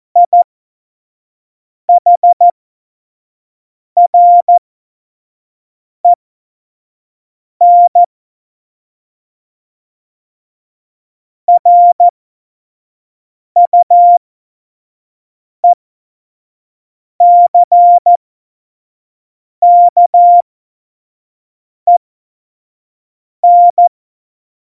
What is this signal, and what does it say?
Morse code, heard as a keyed tone, about 14 wpm, spaced as for 5 wpm, 700 Hz. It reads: IHREN RUECKEN
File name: Morsekurs / Witze / Witz 17.Juli 2018, 25 F70